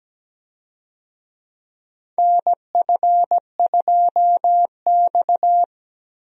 Morse code, heard as a keyed tone, about 17 wpm, 700 Hz. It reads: NF2X
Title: USA Call Signs - 4 Characters - Repeated 2x - 17wpm